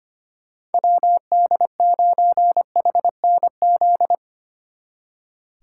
Morse code, heard as a keyed tone, about 34 wpm, 700 Hz. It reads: WD9HNZ